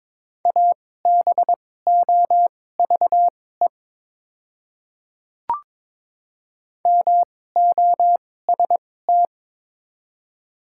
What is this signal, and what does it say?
Morse code, heard as a keyed tone, about 22 wpm, spaced as for 16 wpm, 690 Hz. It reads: ABOVE MOST
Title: 2x Character Spacing - Top 500 Words - 22wpm